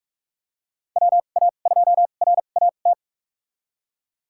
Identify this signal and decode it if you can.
Morse code, heard as a keyed tone, about 42 wpm, spaced as for 31 wpm, 690 Hz. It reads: WA2RAT
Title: USA Call Signs - 6 Characters - 2x Character Spacing - 45wpm